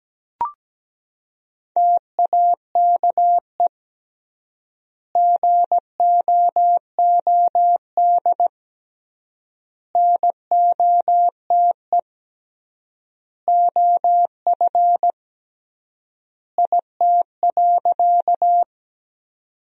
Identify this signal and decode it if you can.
Morse code, heard as a keyed tone, about 17 wpm, 700 Hz. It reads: TAKE GOOD NOTE OF IT.
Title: 3x Word-Spacing - Sentences from Top 500 Words - 17wpm